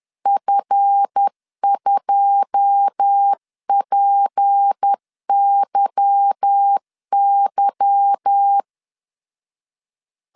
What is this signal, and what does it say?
Morse code, heard as a keyed tone, about 10 wpm, 800 Hz. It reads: F2PYY
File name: CW / DET / 1 9 1500C S B 10wpm 4439